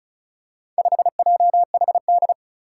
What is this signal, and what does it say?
Morse code, heard as a keyed tone, about 36 wpm, 700 Hz. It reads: 5JHD
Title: Four Challenging Letter-Number Combinations 35wpm